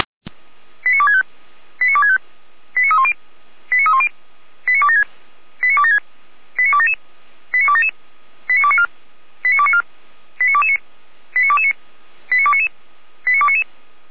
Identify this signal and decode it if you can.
Five-tone selective call (ZVEI), heard as quick runs of five tones, tones of about 70 ms. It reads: 89265 89265 89310 89310 89276 89276 89277 89277 89294 89294 89229 89229 89299 89299
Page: /455/SDR/ZVEI/89250/